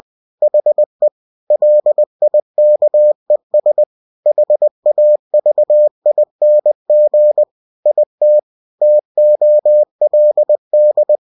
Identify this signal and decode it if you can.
Morse code, heard as a keyed tone, about 20 wpm, 600 Hz. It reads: HE LIKES HAVING IT TOLD
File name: cw-books / winnie-the-pooh / winnie-20wpm-page-0011